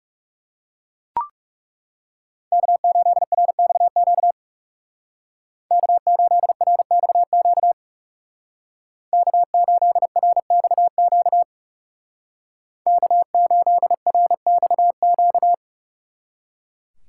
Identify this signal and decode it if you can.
Morse code, heard as a keyed tone, about 37 wpm, 700 Hz. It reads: K8RXQ K8RXQ K8RXQ K8RXQ
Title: Speed Racing - USA 1x3 Call Signs 30wpm